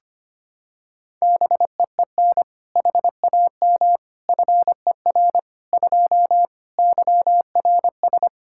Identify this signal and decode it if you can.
Morse code, heard as a keyed tone, about 25 wpm, 700 Hz. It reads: BEEN HAM FER 2 YRS
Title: K1USN SST QSO Practice 25wpm